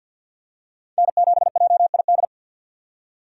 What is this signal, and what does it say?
Morse code, heard as a keyed tone, about 52 wpm, 700 Hz. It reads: N7JID
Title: USA Call Signs - 2x Character Spacing - 50wpm